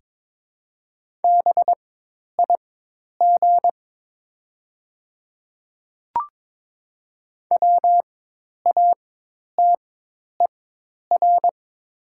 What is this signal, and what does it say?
Morse code, heard as a keyed tone, about 22 wpm, spaced as for 10 wpm, 700 Hz. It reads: BIG WATER
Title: 4x Character Spacing - Top 300 Words - 22wpm